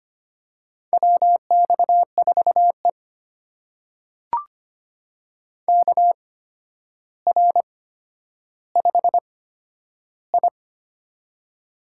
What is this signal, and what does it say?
Morse code, heard as a keyed tone, about 25 wpm, spaced as for 11 wpm, 700 Hz. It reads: WX4E KR5I
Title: USA Call Signs - 4 Characters - 8x Character Spacing - 25wpm